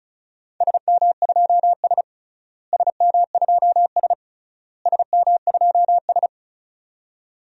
Morse code, heard as a keyed tone, about 34 wpm, 700 Hz. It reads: SM2S SM2S SM2S
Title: DXCC-100 Call Signs - 4 Characters - Repeated 3x - 35wpm